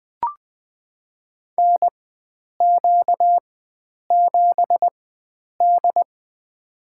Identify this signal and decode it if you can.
Morse code, heard as a keyed tone, about 20 wpm, spaced as for 9 wpm, 700 Hz. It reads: NQ7D
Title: USA Call Signs - 4 Characters - 4x Character Spacing - 20wpm